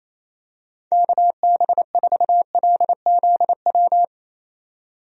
Morse code, heard as a keyed tone, about 27 wpm, 700 Hz. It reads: KB4LZW
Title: USA Call Signs - 6 Characters - 28wpm